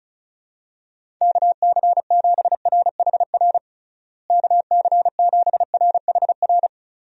Morse code, heard as a keyed tone, about 36 wpm, 700 Hz. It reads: KC7RHR KC7RHR
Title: USA Call Signs - 6 Characters - Repeated 3x - 35wpm